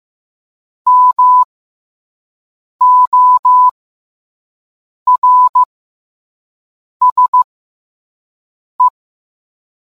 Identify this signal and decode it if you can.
Morse code, heard as a keyed tone, about 15 wpm, spaced as for 5 wpm, 1000 Hz. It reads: MORSE